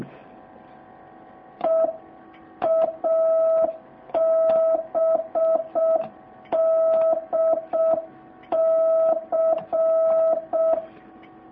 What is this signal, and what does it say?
Morse code, heard as a keyed tone, about 6 wpm, 650 Hz. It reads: EABDC